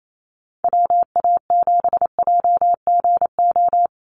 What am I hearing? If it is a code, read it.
Morse code, heard as a keyed tone, about 28 wpm, 700 Hz. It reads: WA7JGO
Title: USA Call Signs 28wpm